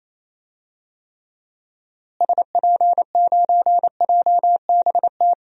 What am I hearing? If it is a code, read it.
Morse code, heard as a keyed tone, about 28 wpm, 700 Hz. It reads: SP9JBT
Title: DXCC-100 Call Signs - 6 Characters - Repeated 3x - 28wpm